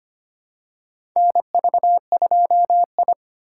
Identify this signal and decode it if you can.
Morse code, heard as a keyed tone, about 25 wpm, 700 Hz. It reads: NV2I
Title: USA Call Signs - 3x Character Spacing - 25wpm